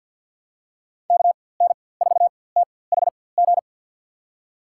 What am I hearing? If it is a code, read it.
Morse code, heard as a keyed tone, about 53 wpm, spaced as for 24 wpm, 700 Hz. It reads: KN4THG
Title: USA Call Signs - 6 Characters - 4x Character Spacing - 50wpm